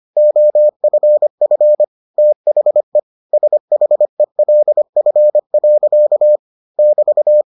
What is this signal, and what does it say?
Morse code, heard as a keyed tone, about 25 wpm, 600 Hz. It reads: OFF THE SHELF. <BT>